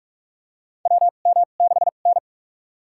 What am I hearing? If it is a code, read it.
Morse code, heard as a keyed tone, about 46 wpm, spaced as for 33 wpm, 700 Hz. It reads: WM6N
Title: USA Call Signs - 2x Character Spacing - 45wpm